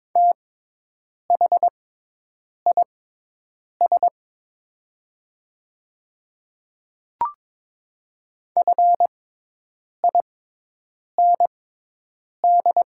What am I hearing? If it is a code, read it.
Morse code, heard as a keyed tone, about 22 wpm, spaced as for 8 wpm, 700 Hz. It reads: THIS FIND
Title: 6x Character Spacing - Top 100 Words - 22wpm